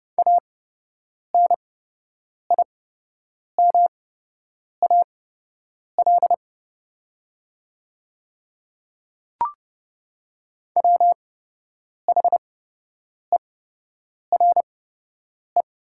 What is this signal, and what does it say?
Morse code, heard as a keyed tone, about 30 wpm, spaced as for 8 wpm, 700 Hz. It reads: ANIMAL WHERE
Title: 8x Character Spacing - Top 500 Words - 30wpm